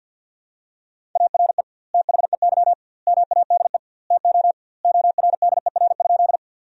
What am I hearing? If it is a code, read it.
Morse code, heard as a keyed tone, about 50 wpm, 700 Hz. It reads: ARE THEY MADE TO ORDER?
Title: 2x Word-Spacing - Sentences from Top 400 Words - 50wpm